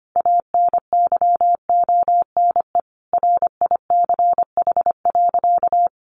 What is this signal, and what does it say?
Morse code, heard as a keyed tone, about 25 wpm, 700 Hz. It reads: ANYONE RICH.
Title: Sentences from Top 1100 Words 25wpm